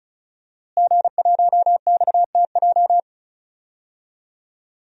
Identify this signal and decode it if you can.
Morse code, heard as a keyed tone, about 34 wpm, 700 Hz. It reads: G1XTJ
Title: DXCC-100 Call Signs - 5 Characters - Repeated 3x - 35wpm